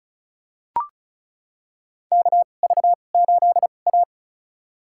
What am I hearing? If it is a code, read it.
Morse code, heard as a keyed tone, about 34 wpm, spaced as for 25 wpm, 700 Hz. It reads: KV8A
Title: USA Call Signs - 4 Characters - 2x Character Spacing - 35wpm